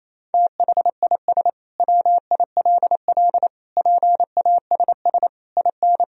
Morse code, heard as a keyed tone, about 28 wpm, 700 Hz. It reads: THIS WILL PASS IN
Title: Sentences from Top 400 Words and Abbreviations 28wpm